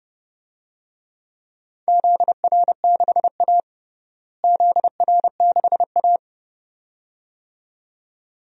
Morse code, heard as a keyed tone, about 30 wpm, 700 Hz. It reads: ZR6A ZR6A
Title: DXCC-100 Call Signs - 4 Characters - Repeated 2x - 30wpm